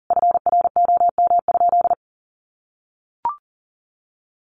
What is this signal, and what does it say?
Morse code, heard as a keyed tone, about 40 wpm, 700 Hz. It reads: FROM?
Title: Sentences from Top 1500 Words 40wpm